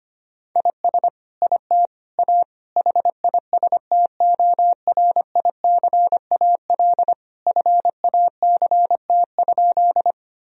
Morse code, heard as a keyed tone, about 25 wpm, 700 Hz. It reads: IS IT A HISTORICAL FACT?
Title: Sentences from Top 1000 Words - Review Part 4 - 25wpm